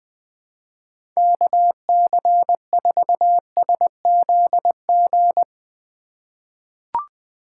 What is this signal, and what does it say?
Morse code, heard as a keyed tone, about 20 wpm, 700 Hz. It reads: KC4SZG
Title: USA Call Signs - 6 Characters - 2x Character Spacing - 20wpm